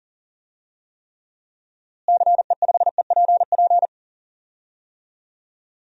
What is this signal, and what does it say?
Morse code, heard as a keyed tone, about 39 wpm, 700 Hz. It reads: CE5EPP
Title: DXCC-100 Call Signs - 6 Characters - Repeated 3x - 40wpm